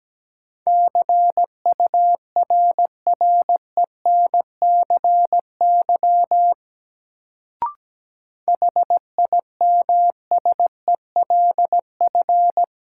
Morse code, heard as a keyed tone, about 17 wpm, 700 Hz. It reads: CURRENCY HIMSELF
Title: Sentences from Top 1500 Words - New Word Drill Alt - 17wpm